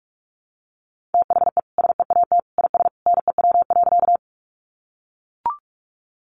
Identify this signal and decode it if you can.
Morse code, heard as a keyed tone, about 44 wpm, 700 Hz. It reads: THE SEAT IS NEW.
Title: Sentences from Top 1500 Words 45wpm